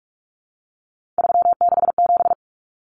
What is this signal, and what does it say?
Morse code, heard as a keyed tone, about 44 wpm, 700 Hz. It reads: 367
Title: Common Cut Numbers - 3-Digit 45wpm